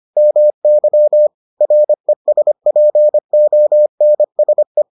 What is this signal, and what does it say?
Morse code, heard as a keyed tone, about 25 wpm, 600 Hz. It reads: MY RESPONSE